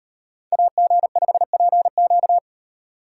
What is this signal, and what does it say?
Morse code, heard as a keyed tone, about 38 wpm, 700 Hz. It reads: AG5PQ